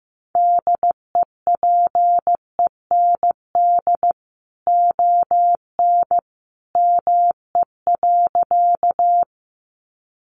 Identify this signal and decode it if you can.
Morse code, heard as a keyed tone, about 15 wpm, 700 Hz. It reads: DEPEND ON ME.
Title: Sentences from Top 1300 Words 15wpm